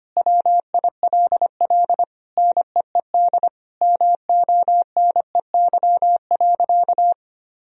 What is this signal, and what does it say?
Morse code, heard as a keyed tone, about 25 wpm, 700 Hz. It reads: WILL NEED MONEY.
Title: Sentences from Top 1000 Words - Review Part 4 - 25wpm